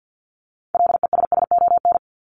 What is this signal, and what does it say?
Morse code, heard as a keyed tone, about 50 wpm, 700 Hz. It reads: LESSON
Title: Top 1001-1250 Words 50wpm